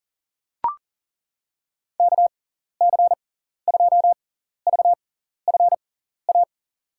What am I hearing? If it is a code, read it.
Morse code, heard as a keyed tone, about 40 wpm, spaced as for 14 wpm, 700 Hz. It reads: KC2VFA
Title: USA Call Signs - 6x Character Spacing - 40wpm